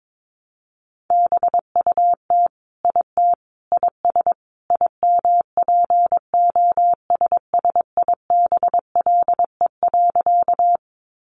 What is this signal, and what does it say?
Morse code, heard as a keyed tone, about 22 wpm, 700 Hz. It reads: BUT IT IS IMPOSSIBLE.
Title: Sentences from Top 1100 Words 22wpm